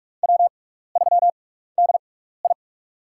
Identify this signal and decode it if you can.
Morse code, heard as a keyed tone, about 46 wpm, spaced as for 16 wpm, 700 Hz. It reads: W3DI
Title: USA Call Signs - 4 Characters - 6x Character Spacing - 45wpm